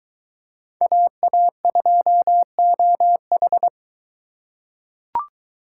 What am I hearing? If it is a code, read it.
Morse code, heard as a keyed tone, about 20 wpm, 700 Hz. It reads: AA2OH